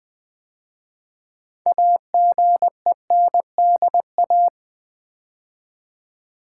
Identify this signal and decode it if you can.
Morse code, heard as a keyed tone, about 20 wpm, 700 Hz. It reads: AGENDA
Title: 6 Letter Words - First 3 Letters Given - 20wpm